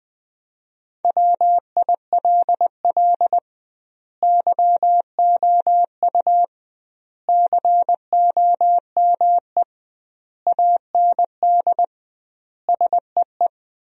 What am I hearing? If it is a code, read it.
Morse code, heard as a keyed tone, about 21 wpm, 700 Hz. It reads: WILL YOU COME AND SEE
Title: 2x Word-Spacing - Sentences from Top 100 Words - 20wpm